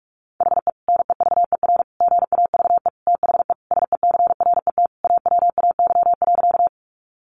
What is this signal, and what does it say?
Morse code, heard as a keyed tone, about 44 wpm, 700 Hz. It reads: HE NEVER GAVE THE SECRET AWAY.
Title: Sentences from Top 1200 Words 45wpm